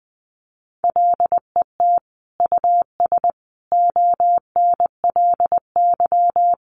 Morse code, heard as a keyed tone, about 20 wpm, 700 Hz. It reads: LET US ONLY
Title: Sentences from Top 800 Words 20wpm